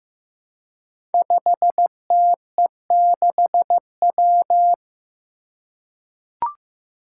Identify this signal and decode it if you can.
Morse code, heard as a keyed tone, about 15 wpm, 700 Hz. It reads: 5TE6W
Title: Five Challenging Letter-Number Combinations 15wpm